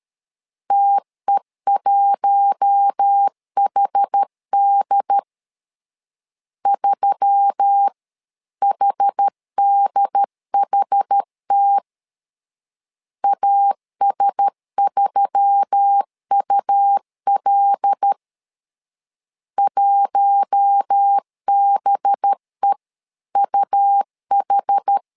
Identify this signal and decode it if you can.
Morse code, heard as a keyed tone, about 13 wpm, 800 Hz. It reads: TE1HD 3HDHT AS3UL 1BEUH
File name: LEZIONI / 5 LBVC / 5 LB 12wpm 1906